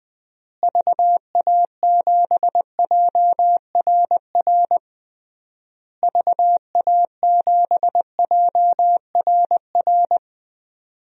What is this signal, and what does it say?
Morse code, heard as a keyed tone, about 20 wpm, 700 Hz. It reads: VA7JRR VA7JRR
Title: DXCC-100 Call Signs - 6 Characters - Repeated 3x - 20wpm